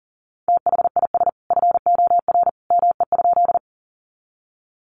Morse code, heard as a keyed tone, about 40 wpm, 700 Hz. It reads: THIS FOR ME?
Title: Sentences from Top 200 Words 40wpm